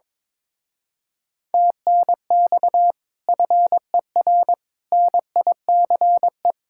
Morse code, heard as a keyed tone, about 22 wpm, 700 Hz. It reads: TNX FER NICE
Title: Speed-Racing Common QSO Phrases - 22wpm